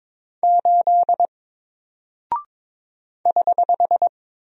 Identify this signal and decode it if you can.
Morse code, heard as a keyed tone, about 22 wpm, 700 Hz. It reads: 8 <HH>
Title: Single Character (Letter:Number:Punctuation:Prosign) Alt 22wpm